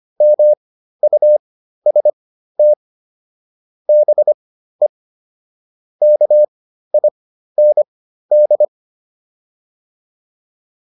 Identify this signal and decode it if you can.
Morse code, heard as a keyed tone, about 25 wpm, spaced as for 13 wpm, 600 Hz. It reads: MUST BE KIND